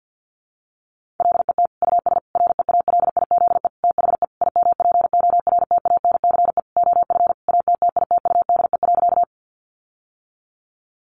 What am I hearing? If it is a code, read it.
Morse code, heard as a keyed tone, about 50 wpm, 700 Hz. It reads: LET US REALIZE THE IMPORTANCE OF ATTITUDE.
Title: Sentences from Top 1300 Words 50wpm